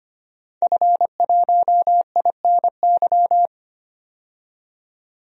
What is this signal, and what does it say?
Morse code, heard as a keyed tone, about 25 wpm, 700 Hz. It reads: F1INY
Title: DXCC-100 Call Signs - 5 Characters - 25wpm